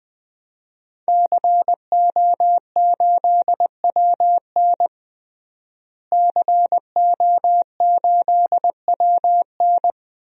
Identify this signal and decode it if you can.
Morse code, heard as a keyed tone, about 20 wpm, 700 Hz. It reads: CO8WN CO8WN
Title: DXCC-100 Call Signs - 5 Characters - Repeated 2x - 20wpm